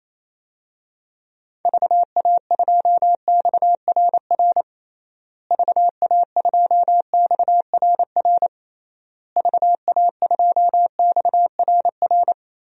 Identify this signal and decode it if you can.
Morse code, heard as a keyed tone, about 28 wpm, 700 Hz. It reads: VA2XRR VA2XRR VA2XRR
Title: DXCC-100 Call Signs - 6 Characters - Repeated 3x - 28wpm